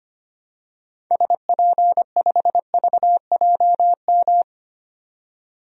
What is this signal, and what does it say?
Morse code, heard as a keyed tone, about 25 wpm, 700 Hz. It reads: SP5VJM
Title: DXCC-100 Call Signs - 6 Characters - 25wpm